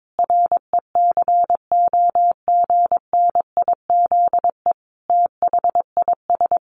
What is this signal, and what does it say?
Morse code, heard as a keyed tone, about 22 wpm, 700 Hz. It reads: RECOGNIZE THIS
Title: Sentences from Top 1400 Words 22wpm